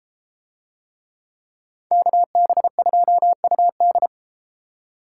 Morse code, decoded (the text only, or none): KB2UD